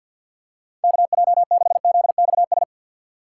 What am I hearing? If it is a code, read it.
Morse code, heard as a keyed tone, about 47 wpm, 700 Hz. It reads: KJ6ZXS
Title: USA Call Signs - 6 Characters - 6x Character Spacing - 50wpm